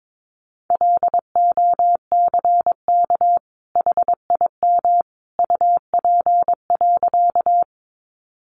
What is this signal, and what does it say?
Morse code, heard as a keyed tone, about 22 wpm, 700 Hz. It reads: LOCK HIM UP.
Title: Sentences from Top 1300 Words 22wpm